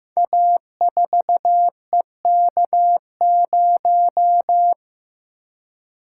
Morse code, heard as a keyed tone, about 15 wpm, 700 Hz. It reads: A4EK0
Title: Code Groups - 5 Letter:Numbers - 15wpm